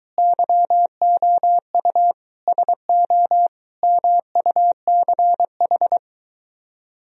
Morse code, heard as a keyed tone, about 23 wpm, 700 Hz. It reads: YOU SO MUCH